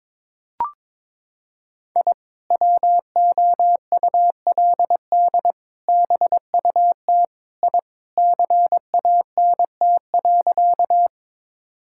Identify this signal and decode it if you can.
Morse code, heard as a keyed tone, about 22 wpm, 700 Hz. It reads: I WOULD BUT I CANT.
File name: practice / 22wpm / Sentences with 2-Word Phrases - 22wpm